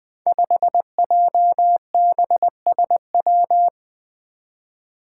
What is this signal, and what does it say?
Morse code, heard as a keyed tone, about 20 wpm, 700 Hz. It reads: 5JBSW
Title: Five Challenging Letter-Number Combinations 20wpm